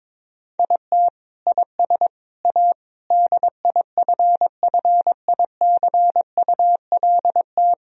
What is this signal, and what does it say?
Morse code, heard as a keyed tone, about 22 wpm, 700 Hz. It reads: IT IS A DIFFICULT
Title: Sentences from Top 1000 Words - Review Part 3 - 22wpm